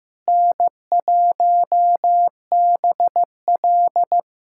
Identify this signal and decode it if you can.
Morse code, heard as a keyed tone, about 16 wpm, 700 Hz. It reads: N1BL
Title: USA Callsign DE Callsign - 4 Characters - 15wpm